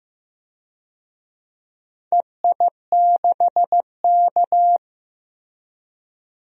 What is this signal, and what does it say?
Morse code, heard as a keyed tone, about 15 wpm, 700 Hz. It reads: EI6K
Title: DXCC-100 Call Signs - 4 Characters - Repeated 2x - 15wpm